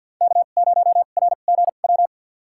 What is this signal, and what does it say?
Morse code, heard as a keyed tone, about 52 wpm, spaced as for 36 wpm, 700 Hz. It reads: K0RGW
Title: USA Call Signs - 5 Characters - 2x Character Spacing - 50wpm